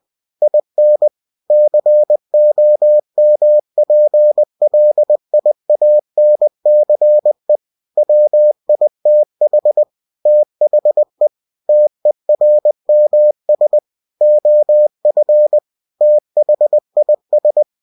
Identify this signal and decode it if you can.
Morse code, heard as a keyed tone, about 20 wpm, 600 Hz. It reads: IN COMPLIANCE WITH THE TERMS OF THIS
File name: cw-books / winnie-the-pooh / winnie-20wpm-page-0114